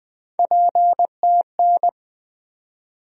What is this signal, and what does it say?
Morse code, heard as a keyed tone, about 20 wpm, 700 Hz. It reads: PTN
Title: USA Call Sign Postfixes - 3 Letters 20wpm